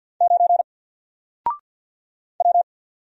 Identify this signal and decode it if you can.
Morse code, heard as a keyed tone, about 50 wpm, 700 Hz. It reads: 9 W
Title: Single Letter-Number 50wpm